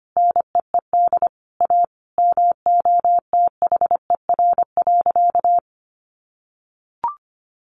Morse code, heard as a keyed tone, about 25 wpm, 700 Hz. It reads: NEED A MOTHER.
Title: Sentences from Top 200 Words 25wpm